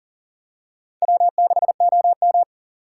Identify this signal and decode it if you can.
Morse code, heard as a keyed tone, about 40 wpm, 700 Hz. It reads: W6OM
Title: USA Call Signs - 4 Characters - 3x Character Spacing - 40wpm